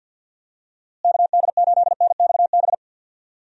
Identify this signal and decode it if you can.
Morse code, heard as a keyed tone, about 51 wpm, 700 Hz. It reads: KD8NXB
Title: USA Call Signs - 6 Characters - Repeated 3x - 50wpm